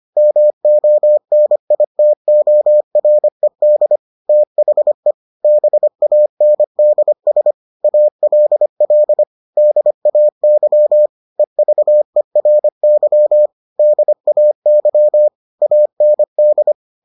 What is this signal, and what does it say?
Morse code, heard as a keyed tone, about 25 wpm, 600 Hz. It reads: MONITORED THE BANDS ALL DAY EVERY DAY AND